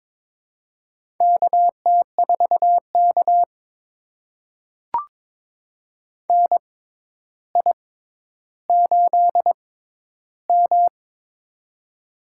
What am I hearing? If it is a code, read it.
Morse code, heard as a keyed tone, about 23 wpm, spaced as for 11 wpm, 700 Hz. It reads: KT4K NI8M
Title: USA Call Signs - 4 Characters - 6x Character Spacing - 22wpm